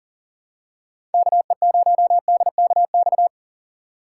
Morse code, heard as a keyed tone, about 40 wpm, 700 Hz. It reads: KE0DKX
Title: USA Call Signs - 6 Characters - 4x Character Spacing - 40wpm